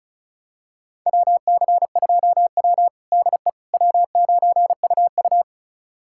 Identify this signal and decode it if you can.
Morse code, heard as a keyed tone, about 36 wpm, 690 Hz. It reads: WC2W DE W9UU
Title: USA Callsign DE Callsign - 4 Characters - 35wpm